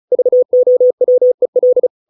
Morse code, heard as a keyed tone, about 35 wpm, 500 Hz. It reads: VOWEL